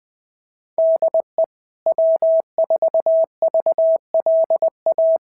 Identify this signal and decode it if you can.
Morse code, heard as a keyed tone, about 20 wpm, 650 Hz. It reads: DE W4VLA
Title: SOTA QSO Practice 20wpm